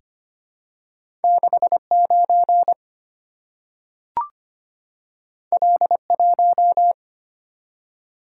Morse code, heard as a keyed tone, about 25 wpm, 700 Hz. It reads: 69 L1
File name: Two Letter-Number Combinations - 25wpm